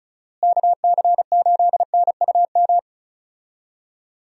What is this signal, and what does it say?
Morse code, heard as a keyed tone, about 34 wpm, 700 Hz. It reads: KC8NUM
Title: USA Call Signs - 6 Characters - Repeated 3x - 35wpm